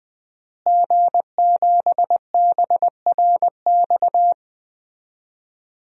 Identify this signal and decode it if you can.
Morse code, heard as a keyed tone, about 21 wpm, 700 Hz. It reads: G7BRX
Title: DXCC-100 Call Signs - 5 Characters - Repeated 2x - 20wpm